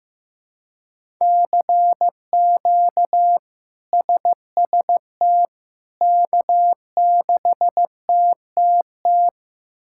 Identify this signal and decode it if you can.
Morse code, heard as a keyed tone, about 16 wpm, 700 Hz. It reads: CQ SST K6TTT
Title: K1USN SST QSO Practice - 15wpm